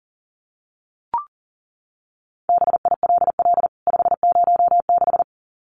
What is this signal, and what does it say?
Morse code, heard as a keyed tone, about 40 wpm, 700 Hz. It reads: BILL 506
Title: CWT Exchange 40wpm